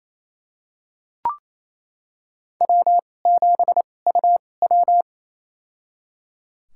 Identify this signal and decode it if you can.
Morse code, heard as a keyed tone, about 28 wpm, spaced as for 20 wpm, 700 Hz. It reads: W7UW
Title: USA Call Signs - 2x Character Spacing - 28wpm